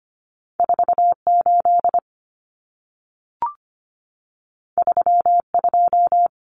Morse code, heard as a keyed tone, about 25 wpm, 700 Hz. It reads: 48 32
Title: Common Cut Numbers - 2-Digit 25wpm